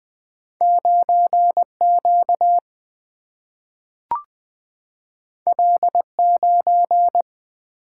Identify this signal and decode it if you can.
Morse code, heard as a keyed tone, about 20 wpm, 700 Hz. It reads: 9 Q L 9
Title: Single-Letter Single-Number 20wpm